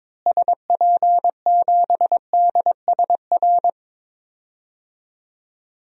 Morse code, heard as a keyed tone, about 22 wpm, 700 Hz. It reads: SP7DSR
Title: DXCC-100 Call Signs - 6 Characters - Repeated 3x - 22wpm